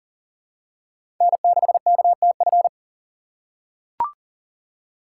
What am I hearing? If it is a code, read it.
Morse code, heard as a keyed tone, about 40 wpm, 700 Hz. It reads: N6KTF
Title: USA Call Signs - 8x Character Spacing - 40wpm